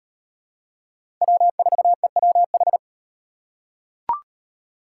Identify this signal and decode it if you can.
Morse code, heard as a keyed tone, about 37 wpm, 700 Hz. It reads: W4EWH